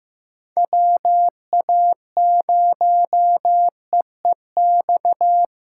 Morse code, heard as a keyed tone, about 15 wpm, 700 Hz. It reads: WA0EEX
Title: USA Call Signs - 6 Characters - 4x Character Spacing - 15wpm